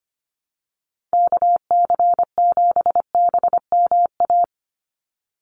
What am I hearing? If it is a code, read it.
Morse code, heard as a keyed tone, about 25 wpm, 700 Hz. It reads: KC7BMA
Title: USA Call Signs 25wpm